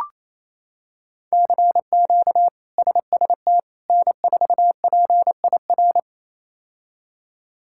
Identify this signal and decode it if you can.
Morse code, heard as a keyed tone, about 28 wpm, 700 Hz. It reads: CQ SST N4PIR
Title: K1USN SST QSO Practice - 28wpm